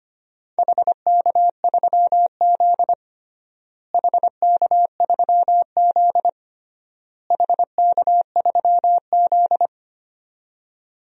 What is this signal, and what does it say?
Morse code, heard as a keyed tone, about 25 wpm, 700 Hz. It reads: HK3Z HK3Z HK3Z
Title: DXCC-100 Call Signs - 4 Characters - Repeated 3x - 25wpm